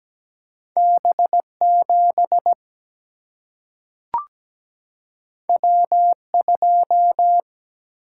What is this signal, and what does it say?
Morse code, heard as a keyed tone, about 17 wpm, 700 Hz. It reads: B7 W2
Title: Single-Letter Single-Number 17wpm